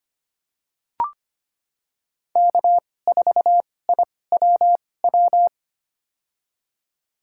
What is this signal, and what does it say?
Morse code, heard as a keyed tone, about 25 wpm, spaced as for 18 wpm, 700 Hz. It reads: K4IWW
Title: USA Call Signs - 5 Characters - 2x Character Spacing - 25wpm